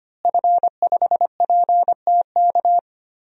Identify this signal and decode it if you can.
Morse code, heard as a keyed tone, about 25 wpm, 700 Hz. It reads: F5PTK
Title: DXCC-100 Call Signs - 5 Characters - Repeated 2x - 25wpm